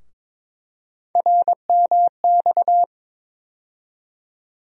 Morse code, heard as a keyed tone, about 22 wpm, 700 Hz. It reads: RMX
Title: USA Call Sign Postfixes - 3 Letters 22wpm